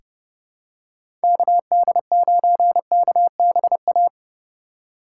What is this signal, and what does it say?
Morse code, heard as a keyed tone, about 30 wpm, 700 Hz. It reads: KD9KBA
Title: USA Call Signs - 6 Characters - 4x Character Spacing - 30wpm